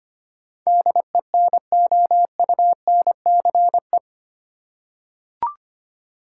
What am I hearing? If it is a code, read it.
Morse code, heard as a keyed tone, about 25 wpm, 690 Hz. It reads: DENOUNCE
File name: 8 Letter Words - First 3 Letters Given - 25wpm